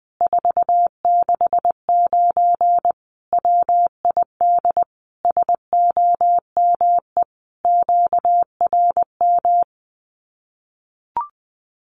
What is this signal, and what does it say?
Morse code, heard as a keyed tone, about 20 wpm, 700 Hz. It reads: 469 WID SOME QRM
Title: Common QSO Phrases 20wpm